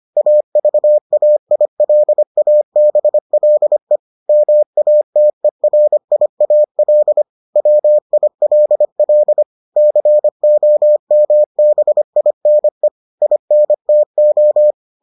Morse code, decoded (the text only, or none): AVAILABLE MATERIAL WILL COMBINE INTO